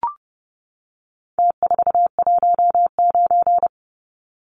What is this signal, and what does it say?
Morse code, heard as a keyed tone, about 30 wpm, 700 Hz. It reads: T419
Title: Numbers - Contest Style - 4-Digit 30wpm